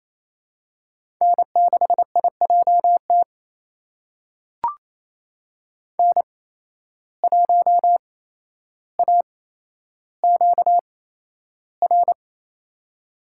Morse code, heard as a keyed tone, about 28 wpm, spaced as for 12 wpm, 700 Hz. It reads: N6IJT N1AQR
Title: USA Call Signs - 5 Characters - 8x Character Spacing - 28wpm